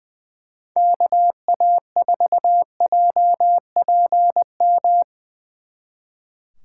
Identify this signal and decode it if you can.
Morse code, heard as a keyed tone, about 23 wpm, 700 Hz. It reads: KA4JPM
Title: Speed Racing - USA 2x3 Call Signs 20wpm